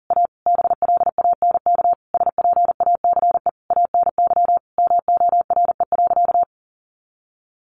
Morse code, heard as a keyed tone, about 40 wpm, 700 Hz. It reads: A BLANK SPACE ANY MORE.
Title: Sentences from Top 1500 Words 40wpm